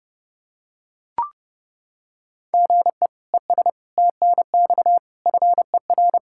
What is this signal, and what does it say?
Morse code, heard as a keyed tone, about 30 wpm, 700 Hz. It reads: GE ES TNX FER
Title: Practice QSO - Exchange 1 - Beginning - 30wpm